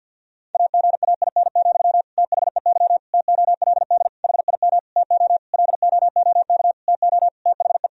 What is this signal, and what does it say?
Morse code, heard as a keyed tone, about 50 wpm, 700 Hz. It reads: AGAIN, THEY TOLD HIM TO LOOK TO THE
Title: Sentences from Top 1000 Words - Review Part 3 - 50wpm